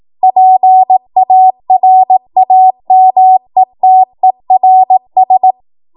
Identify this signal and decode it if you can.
Morse code, heard as a keyed tone, about 18 wpm, 750 Hz. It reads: PARAMETERS
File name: Morse / 18WPM